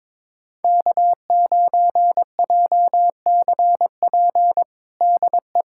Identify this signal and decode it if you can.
Morse code, heard as a keyed tone, about 22 wpm, 700 Hz. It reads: K9JCP DE
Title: USA Callsign DE Callsign 22wpm